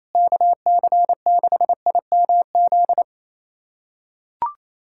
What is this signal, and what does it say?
Morse code, heard as a keyed tone, about 28 wpm, 700 Hz. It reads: KC6IMZ
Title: USA Callsign DE Callsign 28wpm